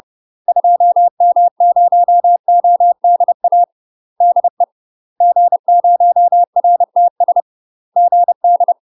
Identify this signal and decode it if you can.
Morse code, heard as a keyed tone, about 30 wpm, 700 Hz. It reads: 2M0ODA DE G0RTS GD
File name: UK MASTER QSOs TRAINING / 30WPM / UK MASTER QSOs 0028